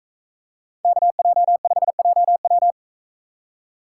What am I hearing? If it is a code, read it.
Morse code, heard as a keyed tone, about 43 wpm, 700 Hz. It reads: KJ5JW